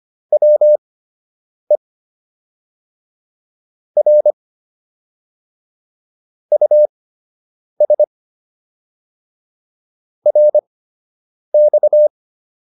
Morse code, heard as a keyed tone, about 25 wpm, spaced as for 8 wpm, 600 Hz. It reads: WE R US RX